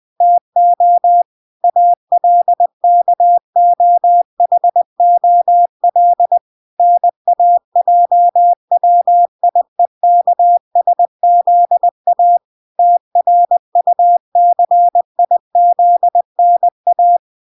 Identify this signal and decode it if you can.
Morse code, heard as a keyed tone, about 20 wpm, 700 Hz. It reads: TO ALKOHOL NAJWIEKSZA TRUCIZNA